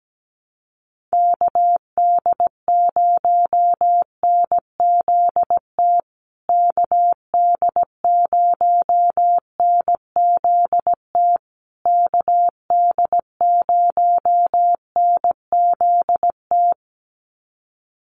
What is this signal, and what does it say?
Morse code, heard as a keyed tone, about 17 wpm, 700 Hz. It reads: KD0NZT KD0NZT KD0NZT
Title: USA Call Signs Repeated 3x 17wpm